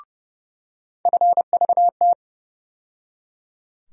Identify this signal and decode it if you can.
Morse code, heard as a keyed tone, about 30 wpm, 700 Hz. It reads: FVT
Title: USA Call Sign Postfixes - 3 Letters 30wpm